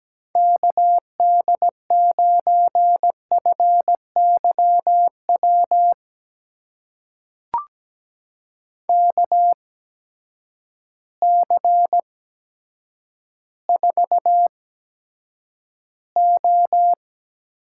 Morse code, heard as a keyed tone, about 17 wpm, spaced as for 9 wpm, 700 Hz. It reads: KD9FYW KC4O
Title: USA Call Signs - 6 Characters - 8x Character Spacing - 17wpm